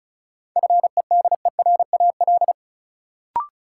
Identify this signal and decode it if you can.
Morse code, heard as a keyed tone, about 36 wpm, 700 Hz. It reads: FEDERAL
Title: Sets of 3 Words - New Word Drill - 35wpm